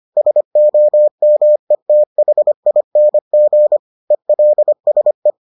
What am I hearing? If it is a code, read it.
Morse code, heard as a keyed tone, about 25 wpm, 600 Hz. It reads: SOMETHING ELSE